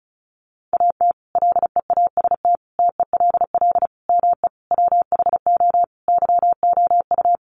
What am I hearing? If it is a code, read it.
Morse code, heard as a keyed tone, about 34 wpm, 700 Hz. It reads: AT LEAST TELL ME WHO YOU
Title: Sentences from Top 700 Words 35wpm